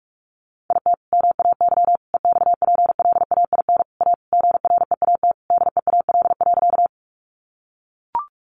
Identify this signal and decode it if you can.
Morse code, heard as a keyed tone, about 45 wpm, 700 Hz. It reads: IT MAY EXPLAIN A GREAT DEAL.
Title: Sentences from Top 1100 Words 45wpm